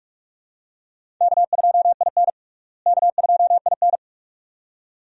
Morse code, heard as a keyed tone, about 43 wpm, 700 Hz. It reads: K2IN K2IN
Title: USA Call Signs - 4 Characters - Repeated 2x - 45wpm